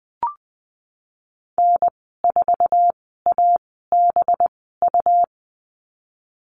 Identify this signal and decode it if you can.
Morse code, heard as a keyed tone, about 20 wpm, spaced as for 15 wpm, 700 Hz. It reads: N4ABU
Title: 193 Lesson 31g - Letter v - Call Signs 20wpm